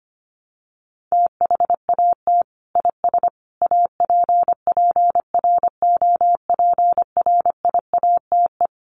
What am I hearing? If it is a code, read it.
Morse code, heard as a keyed tone, about 25 wpm, 700 Hz. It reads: THAT IS APPROPRIATE
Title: Sentences from Top 900 Words 25wpm